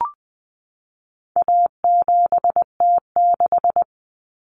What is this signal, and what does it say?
Morse code, heard as a keyed tone, about 20 wpm, 700 Hz. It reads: A7T6
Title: Common Cut Numbers - 4-Digit 20wpm